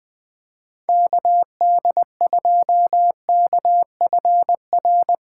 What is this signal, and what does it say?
Morse code, heard as a keyed tone, about 20 wpm, 700 Hz. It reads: KD2KFR
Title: USA Callsign DE Callsign 20wpm